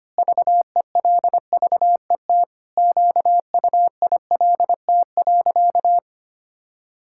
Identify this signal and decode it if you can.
Morse code, heard as a keyed tone, about 25 wpm, 700 Hz. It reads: VELVET QUILT.
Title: Pangram Challenge v2 - 25wpm